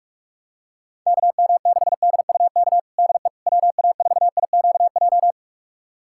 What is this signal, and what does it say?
Morse code, heard as a keyed tone, about 46 wpm, 700 Hz. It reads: KM6DUK DE WA4IQJ
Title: USA Callsign DE Callsign - 6 Characters - 45wpm